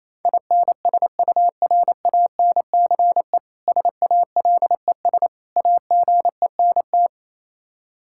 Morse code, heard as a keyed tone, about 27 wpm, 700 Hz. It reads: INSURANCE SALES AGENT
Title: K1USN SST QSO Practice 28wpm